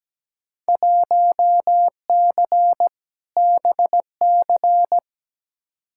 Morse code, heard as a keyed tone, about 17 wpm, 700 Hz. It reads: 1C BC
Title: Field Day Exchanges Encore - 17wpm